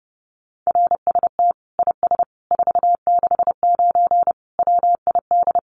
Morse code, heard as a keyed tone, about 30 wpm, 700 Hz. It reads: RST IS 469 WID
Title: Common QSO Phrases 30wpm